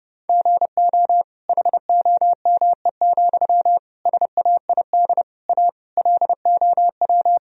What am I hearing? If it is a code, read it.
Morse code, heard as a keyed tone, about 32 wpm, 700 Hz. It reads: GO HOME, SAID A LOW
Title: Sentences from Top 1000 Words - Review Part 1 - 30wpm